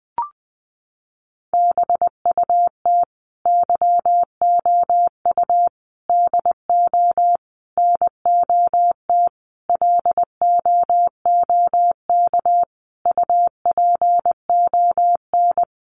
Morse code, heard as a keyed tone, about 19 wpm, 700 Hz. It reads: BUT YOU DO NOT LOOK UPON
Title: Sentences from Top 1100 Words 20wpm